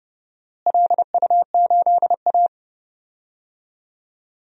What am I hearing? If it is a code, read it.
Morse code, heard as a keyed tone, about 30 wpm, 700 Hz. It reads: LU8A
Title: DXCC-100 Call Signs - 4 Characters - Repeated 3x - 30wpm